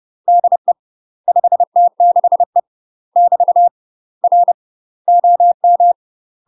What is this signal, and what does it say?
Morse code, heard as a keyed tone, about 30 wpm, 700 Hz. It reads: DE 5T6E <BT> R OM
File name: UK MASTER QSOs TRAINING / 30WPM / UK MASTER QSOs 0024